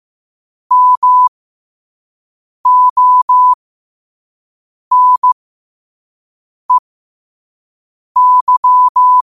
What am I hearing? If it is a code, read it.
Morse code, heard as a keyed tone, about 15 wpm, spaced as for 5 wpm, 1000 Hz. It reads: MONEY